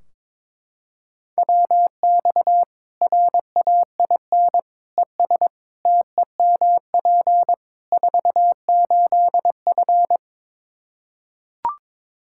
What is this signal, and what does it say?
Morse code, heard as a keyed tone, about 22 wpm, 700 Hz. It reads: WX RAIN ES TEMP 48F
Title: Practice QSO - Exchange 2 - Info - 22wpm